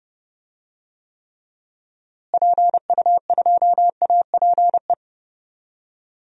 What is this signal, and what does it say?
Morse code, heard as a keyed tone, about 30 wpm, 700 Hz. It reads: PU2APE
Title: DXCC-100 Call Signs - 6 Characters - Repeated 3x - 30wpm